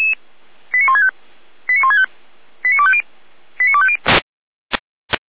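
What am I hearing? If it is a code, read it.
Five-tone selective call (ZVEI), heard as quick runs of five tones, tones of about 70 ms. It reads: 89165 89165 89250 89250
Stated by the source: ZVEI five-tone selective call received on SDR, from the Feuerwehr Orlamuende audio archive.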